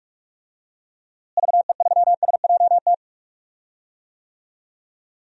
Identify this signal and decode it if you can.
Morse code, heard as a keyed tone, about 46 wpm, 700 Hz. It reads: VE3SJT
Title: DXCC-100 Call Signs - 6 Characters - Repeated 3x - 45wpm